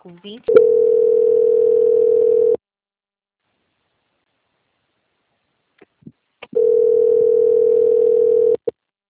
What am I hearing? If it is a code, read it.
Ringback tone (North America), 440 + 480 Hz, with speech.
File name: out- -2048- - - .20846